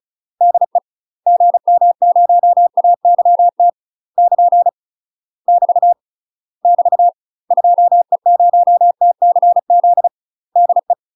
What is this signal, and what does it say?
Morse code, heard as a keyed tone, about 35 wpm, 700 Hz. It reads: DE GM0AYT <KN> <BT> <BT> 2E0TCZ DE
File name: UK MASTER QSOs TRAINING / 35WPM / UK MASTER QSOs 0001